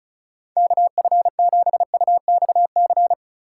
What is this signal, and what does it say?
Morse code, heard as a keyed tone, about 36 wpm, 700 Hz. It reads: KF7UXC
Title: K1USN SST QSO Practice - 35wpm